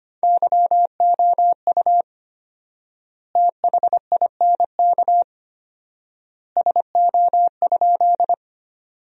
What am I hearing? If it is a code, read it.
Morse code, heard as a keyed tone, about 25 wpm, 700 Hz. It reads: YOU THINK SO?
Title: 4x Word-Spacing - Sentences from Top 200 Words - 25wpm